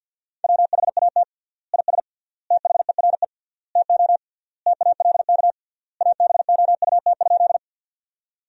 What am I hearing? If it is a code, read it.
Morse code, heard as a keyed tone, about 50 wpm, 700 Hz. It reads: WHAT IS THERE TO TALK ABOUT?
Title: 3x Word-Spacing - Sentences from Top 400 Words - 50wpm